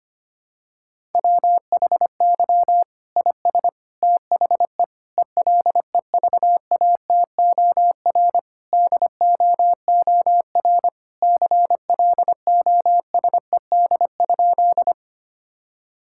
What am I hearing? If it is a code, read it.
Morse code, heard as a keyed tone, about 25 wpm, 700 Hz. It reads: WHY IS THE ELEVATOR DOOR CLOSED?
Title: Sentences from Top 1600 Words 25wpm